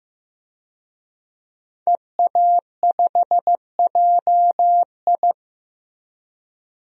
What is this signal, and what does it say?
Morse code, heard as a keyed tone, about 16 wpm, 700 Hz. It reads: EA5JI
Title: DXCC-100 Call Signs - 5 Characters - Repeated 3x - 15wpm